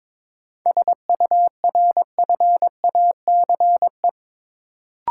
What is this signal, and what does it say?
Morse code, heard as a keyed tone, about 22 wpm, 700 Hz. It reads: SURFACE
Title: Sentences from Top 600 Words - New Word Drill - 22wpm